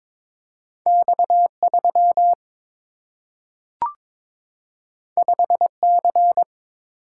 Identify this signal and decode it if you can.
Morse code, heard as a keyed tone, about 22 wpm, 700 Hz. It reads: X 3 5 C
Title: Single-Letter Single-Number 22wpm